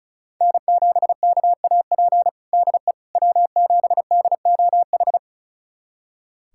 Morse code, heard as a keyed tone, about 36 wpm, 700 Hz. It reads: N7KAP DE W7DOH
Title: USA Callsign DE Callsign - 5 Characters - 35wpm